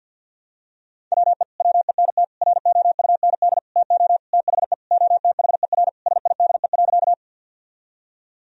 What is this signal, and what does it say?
Morse code, heard as a keyed tone, about 52 wpm, 700 Hz. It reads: WE WENT ROUND TO THE OTHER SIDE.
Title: Sentences from Top 1000 Words - Review Part 1 - 50wpm